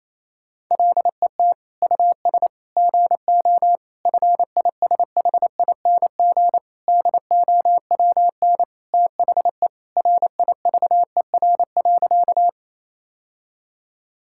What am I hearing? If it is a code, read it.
Morse code, heard as a keyed tone, about 28 wpm, 700 Hz. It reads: LET US GO FISHING DOWN THE RIVER.
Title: Sentences from Top 1000 Words - Review Part 5 - 28wpm